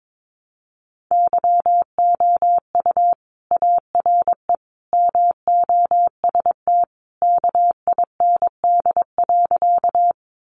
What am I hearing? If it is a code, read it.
Morse code, heard as a keyed tone, about 22 wpm, 700 Hz. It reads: YOU ARE MOST KIND.
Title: Sentences from Top 200 Words 22wpm